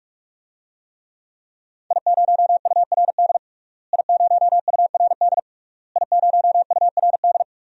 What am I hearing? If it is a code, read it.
Morse code, heard as a keyed tone, about 46 wpm, 700 Hz. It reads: I0URD I0URD I0URD
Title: DXCC-100 Call Signs - 5 Characters - Repeated 3x - 45wpm